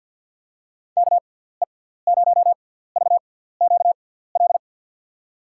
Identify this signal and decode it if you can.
Morse code, heard as a keyed tone, about 53 wpm, spaced as for 17 wpm, 700 Hz. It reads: KE0VQL
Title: USA Call Signs - 6x Character Spacing - 50wpm